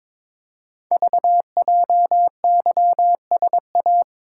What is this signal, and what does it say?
Morse code, heard as a keyed tone, about 22 wpm, 700 Hz. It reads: VJYSA
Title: Code Groups - 5 Letters - 22wpm